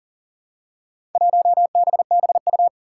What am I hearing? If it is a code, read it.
Morse code, heard as a keyed tone, about 38 wpm, 700 Hz. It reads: 1BBU
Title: Four Challenging Letter-Number Combinations 40wpm